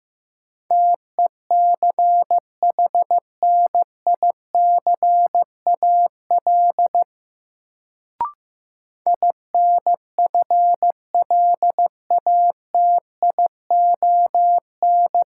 Morse code, heard as a keyed tone, about 15 wpm, 700 Hz. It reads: TECHNICAL INFLATION
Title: Sentences from Top 1100 Words - New Word Drill - 15wpm